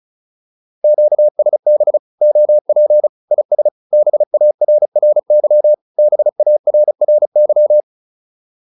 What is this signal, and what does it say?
Morse code, heard as a keyed tone, about 37 wpm, 600 Hz. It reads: QSB OP IS BARRY BARRY